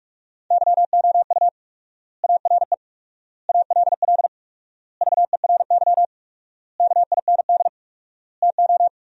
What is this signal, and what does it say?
Morse code, heard as a keyed tone, about 46 wpm, 700 Hz. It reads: YOU ARE ALL VERY KIND TO
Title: 4x Word-Spacing - Sentences from Top 200 Words - 45wpm